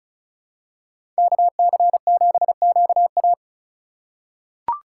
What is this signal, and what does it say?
Morse code, heard as a keyed tone, about 36 wpm, 700 Hz. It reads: KC7QA
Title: USA Call Signs - 5 Characters - 2x Character Spacing - 35wpm